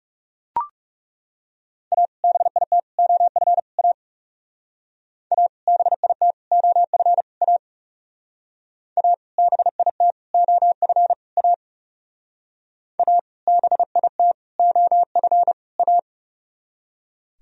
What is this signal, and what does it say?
Morse code, heard as a keyed tone, about 38 wpm, 700 Hz. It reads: A BIT OF A A BIT OF A A BIT OF A A BIT OF A